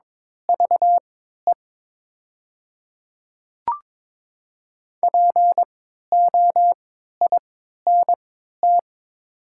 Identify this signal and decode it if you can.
Morse code, heard as a keyed tone, about 22 wpm, spaced as for 12 wpm, 700 Hz. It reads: VE POINT
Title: 3x Character Spacing - Top 500 Words - 22wpm